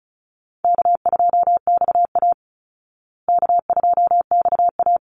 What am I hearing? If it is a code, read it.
Morse code, heard as a keyed tone, about 34 wpm, 700 Hz. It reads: K2XA K2XA
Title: Easy Call Signs Repeated 2x - 35wpm